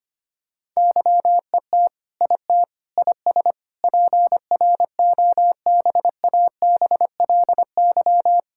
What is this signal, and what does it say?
Morse code, heard as a keyed tone, about 25 wpm, 700 Hz. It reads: YET IT IS PROBABLY
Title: Sentences from Top 1600 Words 25wpm